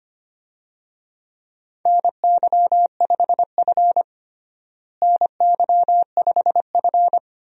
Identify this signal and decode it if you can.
Morse code, heard as a keyed tone, about 25 wpm, 700 Hz. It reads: NY5F NY5F
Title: USA Call Signs - 4 Characters - Repeated 3x - 25wpm